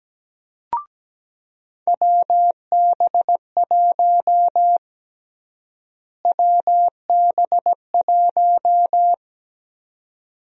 Morse code, heard as a keyed tone, about 17 wpm, 700 Hz. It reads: WB1 WB1
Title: USA Call Sign Prefixes - Repeated 2x - 17wpm